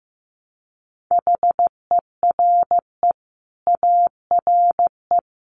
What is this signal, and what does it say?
Morse code, heard as a keyed tone, about 15 wpm, 700 Hz. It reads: HERE ARE
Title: Sets of 5 Words Encore 15wpm